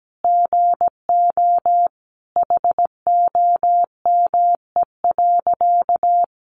Morse code, heard as a keyed tone, about 17 wpm, 700 Hz. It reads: GO HOME.
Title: A Sound Of Thunder 17wpm